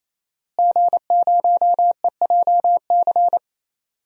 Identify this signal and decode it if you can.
Morse code, heard as a keyed tone, about 28 wpm, 700 Hz. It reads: G0EJC
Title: DXCC-100 Call Signs - 5 Characters - Repeated 3x - 28wpm